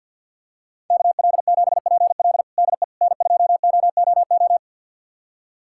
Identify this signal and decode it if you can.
Morse code, heard as a keyed tone, about 52 wpm, 700 Hz. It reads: KL7PL DE N2OOO
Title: USA Callsign DE Callsign - 5 Characters - 50wpm